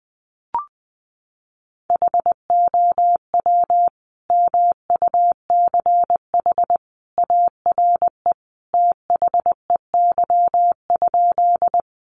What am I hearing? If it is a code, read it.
Morse code, heard as a keyed tone, about 20 wpm, 700 Hz. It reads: HOW MUCH ARE THEY?
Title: Sentences from Top 200 Words 20wpm